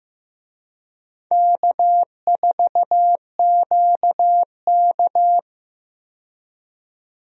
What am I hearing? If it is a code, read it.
Morse code, heard as a keyed tone, about 15 wpm, 700 Hz. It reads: K4QK
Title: USA Call Signs - 4 Characters - 15wpm